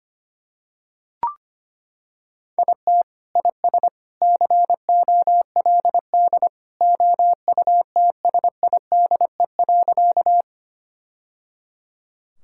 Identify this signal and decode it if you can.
Morse code, heard as a keyed tone, about 25 wpm, 700 Hz. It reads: IT IS COLD OUTSIDE.
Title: Sentences from Top 1000 Words - Review Part 6 - 25wpm